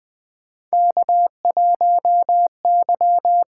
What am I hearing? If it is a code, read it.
Morse code, heard as a keyed tone, about 20 wpm, 700 Hz. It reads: K1Y
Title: USA Call Signs - 4 Characters - 8x Character Spacing - 20wpm